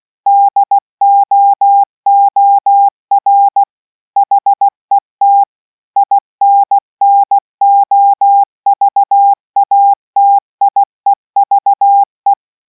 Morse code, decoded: DOOR HET INNOVATIEVE